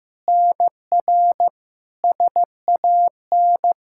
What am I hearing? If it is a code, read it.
Morse code, heard as a keyed tone, about 15 wpm, 700 Hz. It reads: NR SAN
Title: Practice QSO - Exchange 1 - Beginning - 15wpm